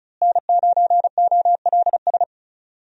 Morse code, heard as a keyed tone, about 36 wpm, 700 Hz. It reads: N9OLS
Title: USA Call Signs - 5 Characters - Repeated 2x - 35wpm